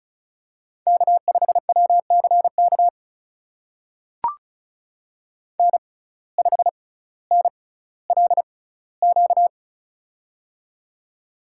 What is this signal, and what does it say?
Morse code, heard as a keyed tone, about 36 wpm, spaced as for 18 wpm, 700 Hz. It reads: K5WCK N5NLQ
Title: USA Call Signs - 5 Characters - 6x Character Spacing - 35wpm